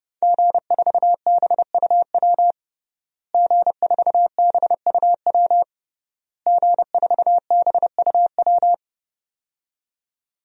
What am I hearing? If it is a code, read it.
Morse code, heard as a keyed tone, about 30 wpm, 700 Hz. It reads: G4BUW G4BUW G4BUW
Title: DXCC-100 Call Signs - 5 Characters - Repeated 3x - 30wpm